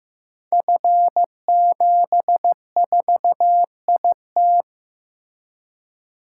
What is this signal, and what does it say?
Morse code, heard as a keyed tone, about 15 wpm, 700 Hz. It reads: F74IT
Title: Code Groups - 5 Letter:Numbers - 15wpm